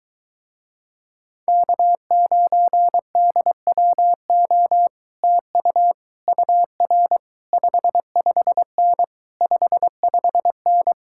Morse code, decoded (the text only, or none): K9DWO TU UR 55N 55N